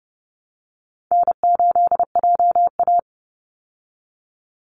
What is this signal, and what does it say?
Morse code, heard as a keyed tone, about 30 wpm, 700 Hz. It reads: N8JA
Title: Call Signs Repeated 2x - 30wpm